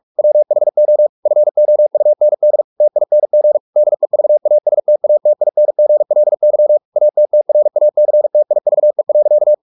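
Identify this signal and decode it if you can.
Morse code, heard as a keyed tone, about 44 wpm, 600 Hz. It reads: WHO FOUND TING DEVASTATINGLY ATTRACTIVE.
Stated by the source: Morse practice at 45 wpm, the novel Never by Ken Follett in English, from CW med Gnister.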